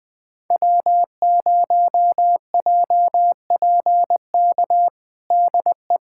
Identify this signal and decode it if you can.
Morse code, heard as a keyed tone, about 19 wpm, 700 Hz. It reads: W0JPK DE
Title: USA Callsign DE Callsign - 5 Characters - 20wpm